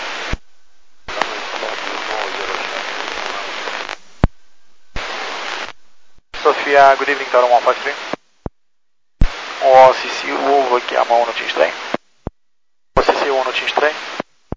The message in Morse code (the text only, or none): none